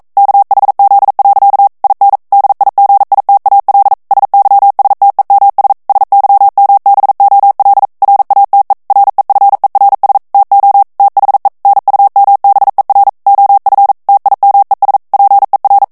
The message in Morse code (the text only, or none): KHZ. IN DIGITAL SYSTEMS SYMBOL RATE REFERS TO THE NUMBER OF TIMES PER